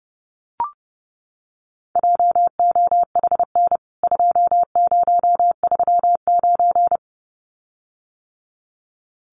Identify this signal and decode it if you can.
Morse code, heard as a keyed tone, about 30 wpm, 700 Hz. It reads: JOHN 2039
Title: CWT Exchange 30wpm